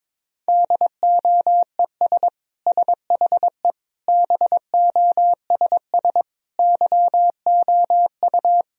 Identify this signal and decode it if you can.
Morse code, heard as a keyed tone, about 22 wpm, 700 Hz. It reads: DOES SHE BOSS YOU
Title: Sentences from Top 1000 Words - Review Part 4 - 22wpm